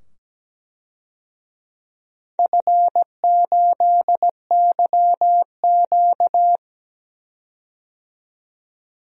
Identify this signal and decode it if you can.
Morse code, heard as a keyed tone, about 17 wpm, 700 Hz. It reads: F8YQ
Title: DXCC-100 Call Signs - 4 Characters - 17wpm